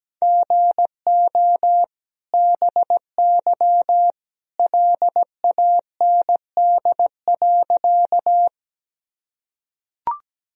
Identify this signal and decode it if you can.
Morse code, heard as a keyed tone, about 17 wpm, 700 Hz. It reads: GO BY LAND.
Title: Sentences from Top 1000 Words - Review Part 1 - 17wpm